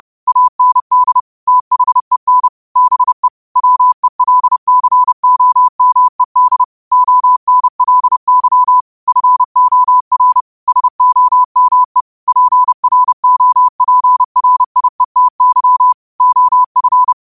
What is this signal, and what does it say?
Morse code, heard as a keyed tone, about 30 wpm, 1000 Hz. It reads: AND THEN BE WELCOMED ONLY FOR SOME PROPRIETY OF
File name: qotdfiles / 30wpm